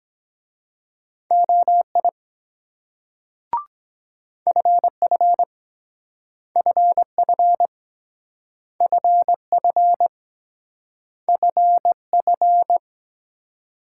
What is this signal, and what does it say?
Morse code, heard as a keyed tone, about 22 wpm, 700 Hz. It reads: OI FF FF FF FF